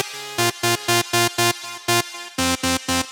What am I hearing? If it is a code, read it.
There is no code.